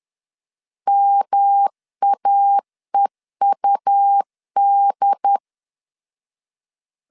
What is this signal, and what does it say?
Morse code, heard as a keyed tone, about 10 wpm, 800 Hz. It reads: MAEUD
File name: CW / DET / 1 3 1500C S C 10wpm 3335